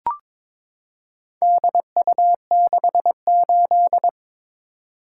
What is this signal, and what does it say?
Morse code, heard as a keyed tone, about 21 wpm, 700 Hz. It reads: DU68
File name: practice / 22wpm / Four Letter-Number Combinations - 22wpm